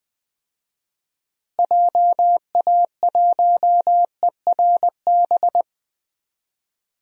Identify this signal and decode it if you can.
Morse code, heard as a keyed tone, about 20 wpm, 700 Hz. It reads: JA1ERB
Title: DXCC-100 Call Signs - 6 Characters - Repeated 3x - 20wpm